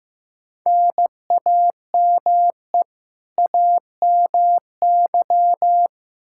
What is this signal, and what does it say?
Morse code, heard as a keyed tone, about 15 wpm, 700 Hz. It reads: NAME AMY
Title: Practice QSO - Exchange 1 - Beginning - 15wpm